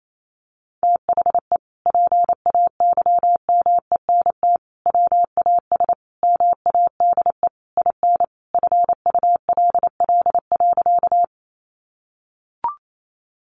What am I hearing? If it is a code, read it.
Morse code, heard as a keyed tone, about 28 wpm, 700 Hz. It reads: THE PAYMENT WAS MADE IN FULL.
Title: Sentences from Top 1200 Words 28wpm